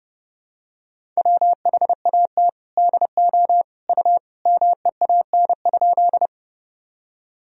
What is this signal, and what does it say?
Morse code, heard as a keyed tone, about 30 wpm, 700 Hz. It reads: WHAT DO U MEAN?
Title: Sentences from Top 200 Words and Abbreviations 30wpm